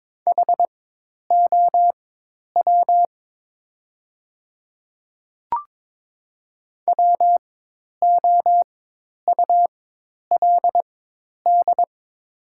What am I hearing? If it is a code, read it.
Morse code, heard as a keyed tone, about 22 wpm, spaced as for 10 wpm, 700 Hz. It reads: HOW WOULD
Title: 4x Character Spacing - Top 200 Words - 22wpm